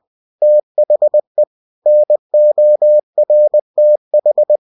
Morse code, heard as a keyed tone, about 20 wpm, 600 Hz. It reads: THE NORTH
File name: cw-books / winnie-the-pooh / winnie-20wpm-page-0093